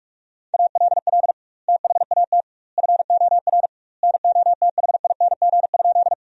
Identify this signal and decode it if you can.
Morse code, heard as a keyed tone, about 44 wpm, 700 Hz. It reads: ALL THAT FOR NOTHING?
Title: 2x Word-Spacing - Sentences from Top 500 Words - 45wpm